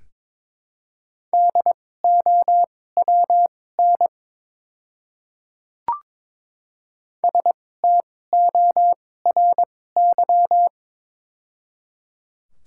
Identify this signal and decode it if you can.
Morse code, heard as a keyed tone, about 22 wpm, spaced as for 16 wpm, 700 Hz. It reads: DOWN STORY
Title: 2x Character Spacing - Top 300 Words - 22wpm